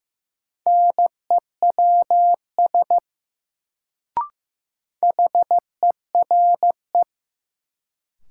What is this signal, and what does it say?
Morse code, heard as a keyed tone, about 15 wpm, 700 Hz. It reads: NEWS HERE
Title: Sets of 5 Words - New Word Drill - 15wpm